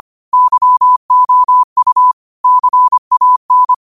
Morse code, heard as a keyed tone, about 25 wpm, 1000 Hz. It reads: YOU CAN